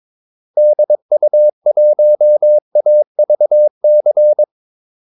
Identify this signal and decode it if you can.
Morse code, heard as a keyed tone, about 22 wpm, 600 Hz. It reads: DU1AVC
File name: Callsigns-22wpm